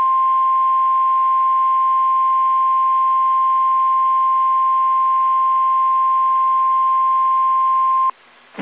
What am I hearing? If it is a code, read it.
Emergency Alert System (SAME) tones (the 1050 Hz attention tone). It is heard from 0.6 to 7.6 s.
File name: eas / KSDZKDJL / KSDZKDJL